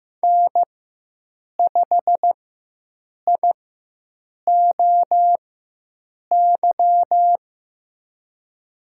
Morse code, heard as a keyed tone, about 15 wpm, spaced as for 7 wpm, 700 Hz. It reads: N5IOY
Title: USA Call Signs - 5 Characters - 4x Character Spacing - 15wpm